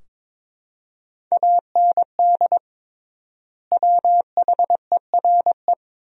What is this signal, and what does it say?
Morse code, heard as a keyed tone, about 22 wpm, 700 Hz. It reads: AND WHERE
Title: 3x Word-Spacing - Sentences from Top 200 Words - 22wpm